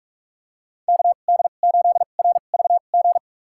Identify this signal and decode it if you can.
Morse code, heard as a keyed tone, about 46 wpm, spaced as for 32 wpm, 700 Hz. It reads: KD8RVG
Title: USA Call Signs - 6 Characters - 2x Character Spacing - 45wpm